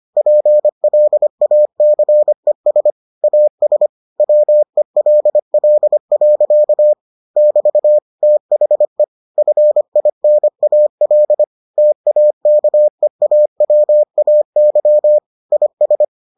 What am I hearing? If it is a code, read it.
Morse code, heard as a keyed tone, about 25 wpm, 600 Hz. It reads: PLACES AS WELL. <BT> THE FINAL TAKEAWAY IS